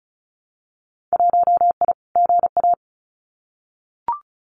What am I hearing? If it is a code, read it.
Morse code, heard as a keyed tone, about 34 wpm, 700 Hz. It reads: 1I GA
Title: Winter Field Day Exchanges Encore 35wpm